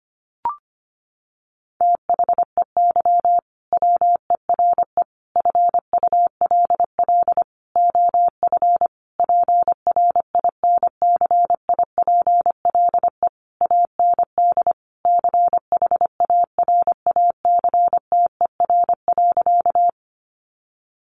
Morse code, heard as a keyed tone, about 25 wpm, 700 Hz. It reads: THEY WERE FULL OF PRINCIPLE AND CHARACTER.
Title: Sentences from Top 1400 Words 25wpm